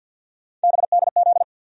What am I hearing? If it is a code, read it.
Morse code, heard as a keyed tone, about 51 wpm, 700 Hz. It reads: BDZ
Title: USA Call Sign Postfixes - 3 Letters 50wpm